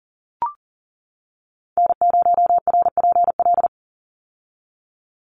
Morse code, heard as a keyed tone, about 40 wpm, 700 Hz. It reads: N0RPL
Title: USA Call Signs 40wpm